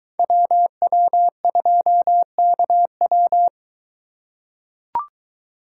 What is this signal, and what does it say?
Morse code, heard as a keyed tone, about 22 wpm, 700 Hz. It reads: WW2KW